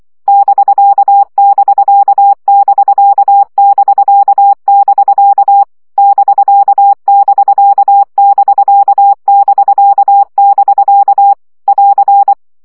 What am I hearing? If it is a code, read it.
Morse code, heard as a keyed tone, about 24 wpm, 800 Hz. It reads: <BK><BK><BK><BK><BK> <BK><BK><BK><BK><BK> <AR>
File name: CW gyakorlas bk 20 baud